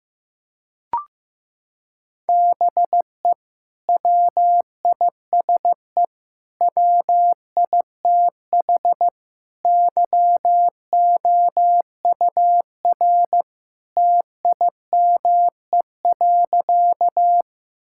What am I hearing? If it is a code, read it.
Morse code, heard as a keyed tone, about 14 wpm, 700 Hz. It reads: BE WISE WITH YOUR TIME.
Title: Sentences from Top 1600 Words 15wpm